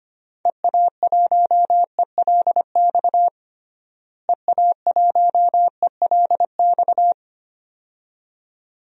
Morse code, heard as a keyed tone, about 25 wpm, 700 Hz. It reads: EA1ELX EA1ELX
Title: DXCC-100 Call Signs - 6 Characters - Repeated 2x - 25wpm